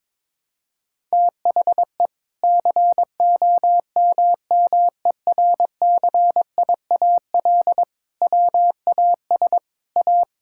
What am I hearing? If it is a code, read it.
Morse code, heard as a keyed tone, about 22 wpm, 700 Hz. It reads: THE COMMERCIAL WAS A
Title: Sentences from Top 1000 Words - Review Part 5 - 22wpm